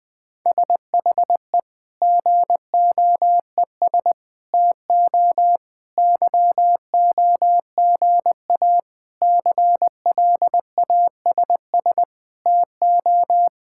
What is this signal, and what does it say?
Morse code, heard as a keyed tone, about 20 wpm, 700 Hz. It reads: SHE GOES TO YOGA CLASS TO
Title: Difficult Sentences with Hints - 20wpm